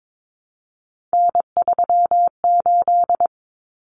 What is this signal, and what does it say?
Morse code, heard as a keyed tone, about 22 wpm, 700 Hz. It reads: N38
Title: Common Cut Numbers - 3-Digit 22wpm